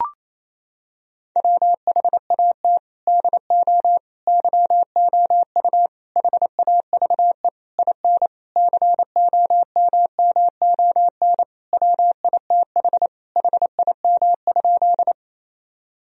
Morse code, heard as a keyed tone, about 28 wpm, 700 Hz. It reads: WHAT DO YOU HAVE IN COMMON WITH HIM?
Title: Sentences from Top 1000 Words - Review Part 3 - 28wpm